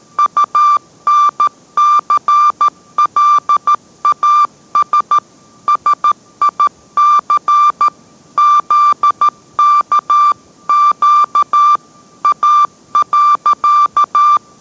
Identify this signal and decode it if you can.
Morse code, heard as a keyed tone, about 13 wpm, 1180 Hz. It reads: UNCLAS SIC ZKQ A.